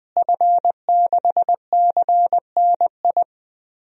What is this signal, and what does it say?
Morse code, heard as a keyed tone, about 20 wpm, 700 Hz. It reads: F6CNI
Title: DXCC-100 Call Signs - 5 Characters - Repeated 2x - 20wpm